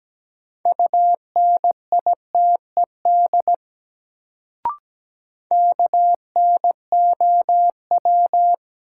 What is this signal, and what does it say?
Morse code, heard as a keyed tone, about 17 wpm, 700 Hz. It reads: UNITED KNOW
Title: Sets of 2 Words - New Word Drill - 17wpm